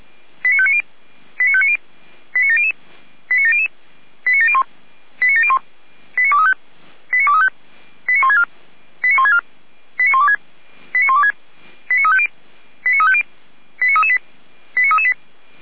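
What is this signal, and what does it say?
Five-tone selective call (ZVEI), heard as quick runs of five tones, tones of about 70 ms. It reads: 89590 89590 89700 89700 89701 89701 89235 89235 89164 89164 89126 89126 89350 89350 89338 89338
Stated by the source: ZVEI five-tone selective call received on SDR.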